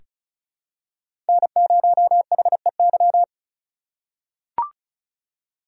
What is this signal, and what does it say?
Morse code, heard as a keyed tone, about 36 wpm, 700 Hz. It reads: N0HEY
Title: USA Call Signs - 5 Characters - 3x Character Spacing - 35wpm